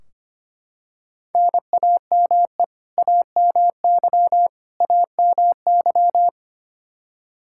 Morse code, heard as a keyed tone, about 25 wpm, 700 Hz. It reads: NAME AMY AMY
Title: Practice QSO - Exchange 1 - Beginning - 25wpm